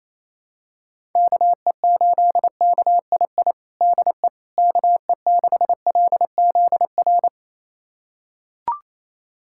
Morse code, heard as a keyed tone, about 28 wpm, 700 Hz. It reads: KE8KII DE KE6LZR
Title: USA Callsign DE Callsign - 6 Characters - 28wpm